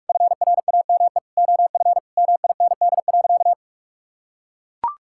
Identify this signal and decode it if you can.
Morse code, heard as a keyed tone, about 44 wpm, 700 Hz. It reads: FRAME OF MIND.
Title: Sentences from Top 1000 Words - Review Part 6 - 45wpm